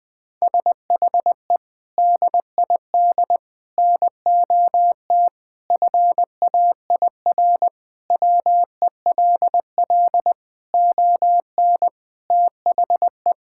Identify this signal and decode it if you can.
Morse code, heard as a keyed tone, about 20 wpm, 700 Hz. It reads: SHE DID NOT FAIR WELL ON THE
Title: Sentences from Top 1000 Words - Review Part 5 - 20wpm